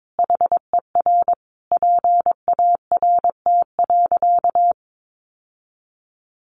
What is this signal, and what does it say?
Morse code, heard as a keyed tone, about 22 wpm, 700 Hz. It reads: HER PART.
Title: Sentences from Top 1200 Words 22wpm